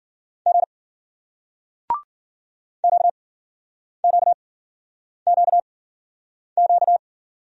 Morse code, heard as a keyed tone, about 58 wpm, spaced as for 9 wpm, 700 Hz. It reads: G QQQQ